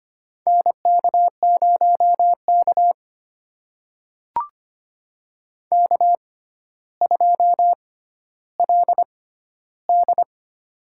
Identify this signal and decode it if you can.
Morse code, heard as a keyed tone, about 25 wpm, spaced as for 13 wpm, 700 Hz. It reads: NK0K K2LD
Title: USA Call Signs - 6x Character Spacing - 25wpm